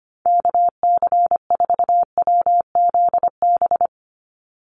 Morse code, heard as a keyed tone, about 25 wpm, 700 Hz. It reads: KC4WZB
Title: NA SOTA Activator Callsigns 25wpm